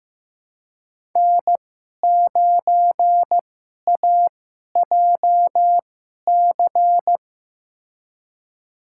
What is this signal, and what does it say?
Morse code, heard as a keyed tone, about 15 wpm, spaced as for 11 wpm, 700 Hz. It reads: N9AJC
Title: USA Call Signs - 2x Character Spacing - 15wpm